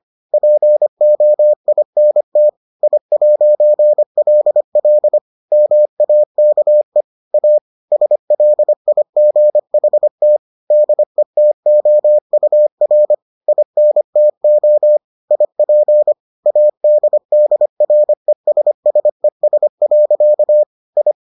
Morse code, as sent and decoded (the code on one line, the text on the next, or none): .--. --- .. -. - / .. .----. .-.. .-.. / -- .- -.- . / .- / ... .-.. .. --. .... - / -.. - --- ..- .-. / .. -. - --- / .. .--. / .- -.. -.. .-. ... ... ... .-.-.- / ..
POINT I'LL MAKE A SLIGHT DETOUR INTO IP ADDRESSES. I